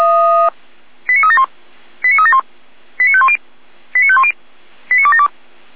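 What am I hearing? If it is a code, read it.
Five-tone selective call (ZVEI), heard as quick runs of five tones, tones of about 70 ms. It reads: 89371 89371 89510 89510 89272
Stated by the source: Five-tone ZVEI selective call 89262 received on SDR.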